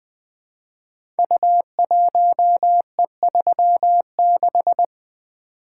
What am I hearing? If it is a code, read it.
Morse code, heard as a keyed tone, about 20 wpm, 700 Hz. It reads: U1E36
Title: Code Groups - 5 Letter:Numbers - 20wpm